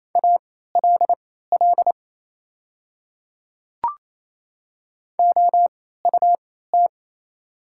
Morse code, heard as a keyed tone, about 28 wpm, spaced as for 16 wpm, 700 Hz. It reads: ALL OUT
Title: 3x Character Spacing - Top 100 Words - 28wpm